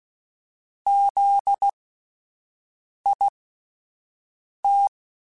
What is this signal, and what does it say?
Morse code, heard as a keyed tone, about 16 wpm, spaced as for 5 wpm, 800 Hz. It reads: ZIT
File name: Alphabetical / Alphabetical mp3s / Zs @5